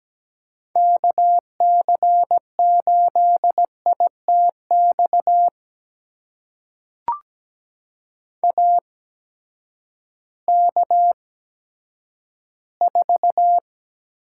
Morse code, heard as a keyed tone, about 17 wpm, spaced as for 10 wpm, 700 Hz. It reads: KC8ITX AK4
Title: USA Call Signs - 8x Character Spacing - 17wpm